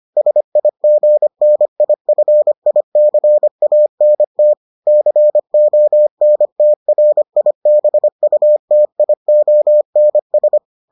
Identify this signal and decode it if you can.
Morse code, heard as a keyed tone, about 25 wpm, 600 Hz. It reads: SIGNIFICANT CONTRIBUTIONS